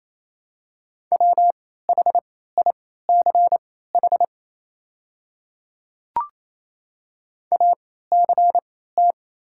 Morse code, heard as a keyed tone, about 28 wpm, spaced as for 16 wpm, 700 Hz. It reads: WHICH ACT
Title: 3x Character Spacing - Top 200 Words - 28wpm